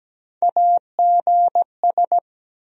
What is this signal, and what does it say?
Morse code, heard as a keyed tone, about 17 wpm, 700 Hz. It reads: AGS
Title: USA Call Sign Postfixes - 3 Letters 17wpm